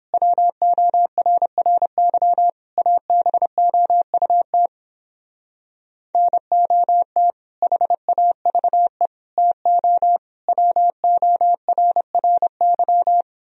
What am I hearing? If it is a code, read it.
Morse code, heard as a keyed tone, about 29 wpm, 700 Hz. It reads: WORRY ABOUT NOT HAVE TO WORRY